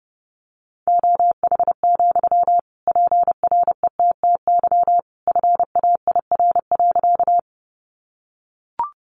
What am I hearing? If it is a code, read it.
Morse code, heard as a keyed tone, about 30 wpm, 700 Hz. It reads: OH, PRETTY FAIR.
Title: Sentences from Top 1300 Words 30wpm